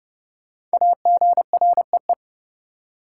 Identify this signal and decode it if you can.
Morse code, heard as a keyed tone, about 30 wpm, 700 Hz. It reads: AGREE
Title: Sentences from Top 1000 Words - Review Part 6 - New Word Drill - 30wpm